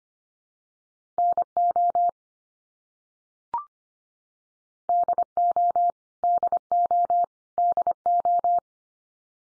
Morse code, heard as a keyed tone, about 25 wpm, 700 Hz. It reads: NO DO DO DO
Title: Top 100 Words 25wpm x3